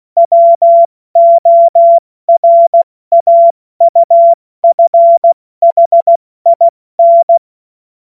Morse code, heard as a keyed tone, about 16 wpm, spaced as for 14 wpm, 680 Hz. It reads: WORAUFHIN
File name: Morsekurs / Witze / Witz 26. Oktober 2021, 70 F80